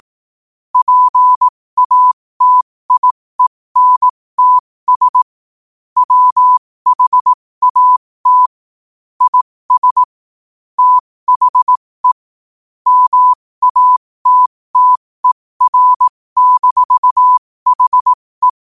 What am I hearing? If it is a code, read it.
Morse code, heard as a keyed tone, about 18 wpm, spaced as for 15 wpm, 1000 Hz. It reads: PATIENTS WHAT IS THE MATTER-HE
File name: qotdfiles / 15wpm